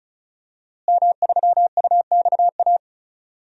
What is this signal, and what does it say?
Morse code, heard as a keyed tone, about 36 wpm, 700 Hz. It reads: M3UXA
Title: DXCC-100 Call Signs - 5 Characters - Repeated 2x - 35wpm